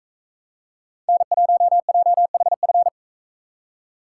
Morse code, heard as a keyed tone, about 42 wpm, 700 Hz. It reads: N1JHF